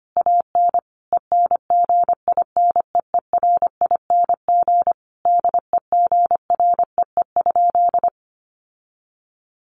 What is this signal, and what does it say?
Morse code, heard as a keyed tone, about 25 wpm, 700 Hz. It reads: AN ENGINEERING DEGREE?
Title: Sentences from Top 1500 Words 25wpm